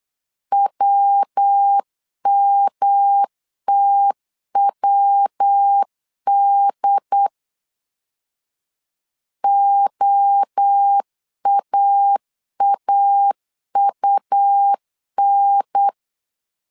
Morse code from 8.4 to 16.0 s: OAAUN